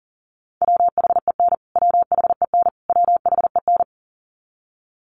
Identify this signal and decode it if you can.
Morse code, heard as a keyed tone, about 40 wpm, 700 Hz. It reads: WHEN WHEN WHEN
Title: Top 500 Words Repeated 40wpm